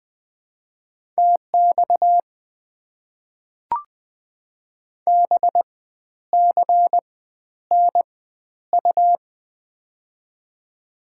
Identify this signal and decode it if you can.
Morse code, heard as a keyed tone, about 20 wpm, spaced as for 11 wpm, 690 Hz. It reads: TX BCNU
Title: Top 90 QSO Elements - 4x Character Spacing - 20wpm